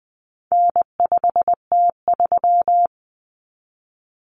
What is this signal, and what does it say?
Morse code, heard as a keyed tone, about 20 wpm, 700 Hz. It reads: N5T3
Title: Common Cut Numbers - 4-Digit 20wpm